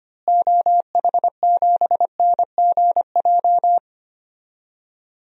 Morse code, heard as a keyed tone, about 25 wpm, 700 Hz. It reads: OH7NGJ
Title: DXCC-100 Call Signs - 6 Characters - Repeated 3x - 25wpm